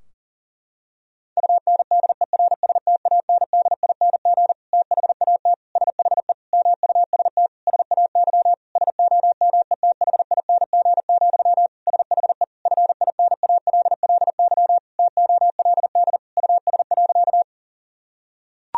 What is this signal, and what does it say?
Morse code, heard as a keyed tone, about 37 wpm, 700 Hz. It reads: UNDERSTANDING THAT SHE MUST SAY SOMETHING, SHE FINALLY TOLD US.
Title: Sentences from Top 1000 Words - Review Part 3 - 40wpm